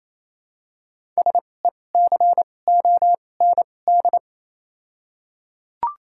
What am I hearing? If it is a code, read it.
Morse code, heard as a keyed tone, about 28 wpm, spaced as for 20 wpm, 700 Hz. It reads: SECOND
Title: 2x Character Spacing - Top 300 Words - 28wpm